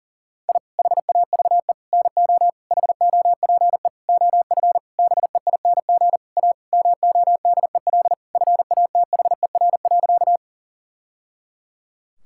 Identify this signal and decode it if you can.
Morse code, heard as a keyed tone, about 40 wpm, 700 Hz. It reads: I HAVE NO HOPE OF BEING A MODEL FATHER.
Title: Sentences from Top 1000 Words - Review Part 5 - 40wpm